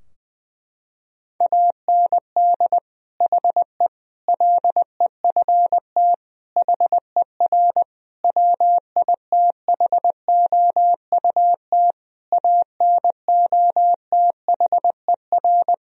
Morse code, heard as a keyed tone, about 20 wpm, 700 Hz. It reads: AND HE LEFT HER WITHOUT ANOTHER
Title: Sentences from Top 1000 Words - Review Part 3 - 20wpm